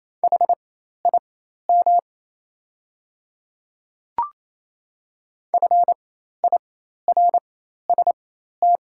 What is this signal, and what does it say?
Morse code, heard as a keyed tone, about 28 wpm, spaced as for 13 wpm, 700 Hz. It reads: HIM FIRST
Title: 4x Character Spacing - Top 100 Words - 28wpm